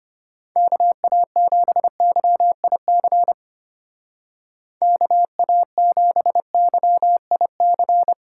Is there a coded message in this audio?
Yes: Morse code from 0.6 to 8.1 s, heard as a keyed tone, about 31 wpm, 700 Hz. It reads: KA7YIC KA7YIC